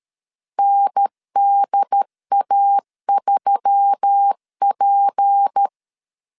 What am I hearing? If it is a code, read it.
Morse code, heard as a keyed tone, about 13 wpm, 800 Hz. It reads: NDA3P
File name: CW / DET / 1 6 1500C S B 12wpm 3431